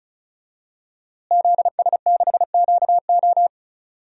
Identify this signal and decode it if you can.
Morse code, heard as a keyed tone, about 36 wpm, 700 Hz. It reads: ZS6QO
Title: DXCC-100 Call Signs - 5 Characters - Repeated 3x - 35wpm